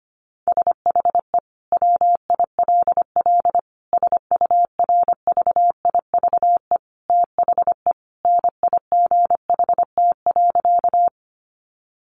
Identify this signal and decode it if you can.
Morse code, heard as a keyed tone, about 25 wpm, 700 Hz. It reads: SHE WILL SURVIVE THE NIGHT.
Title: Sentences from Top 1300 Words 25wpm